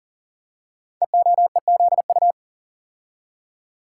Morse code, heard as a keyed tone, about 39 wpm, 690 Hz. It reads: EOEZU
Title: Code Groups - 5 Letters - 40wpm